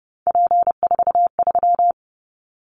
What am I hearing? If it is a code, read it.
Morse code, heard as a keyed tone, about 32 wpm, 700 Hz. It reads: P43
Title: Callsign Prefixes for Top 100 DXCC - 30wpm